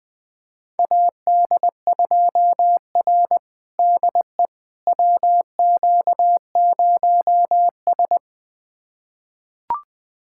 Morse code, heard as a keyed tone, about 21 wpm, 700 Hz. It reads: AD2R DE WQ0S
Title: USA Callsign DE Callsign - 4 Characters - 20wpm